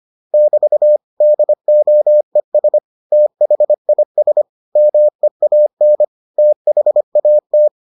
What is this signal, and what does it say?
Morse code, heard as a keyed tone, about 25 wpm, 600 Hz. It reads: <BT> DOES THIS MEAN THAT